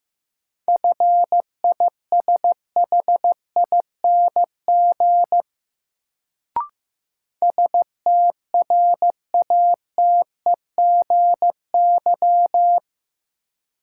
Morse code, heard as a keyed tone, about 15 wpm, 700 Hz. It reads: FISHING STRATEGY
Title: Sentences from Top 1000 Words - Review Part 5 - New Word Drill - 15wpm